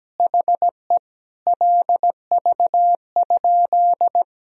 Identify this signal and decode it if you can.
Morse code, heard as a keyed tone, about 17 wpm, 700 Hz. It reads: HE LV?
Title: Sentences from Top 400 Words and Abbreviations 17wpm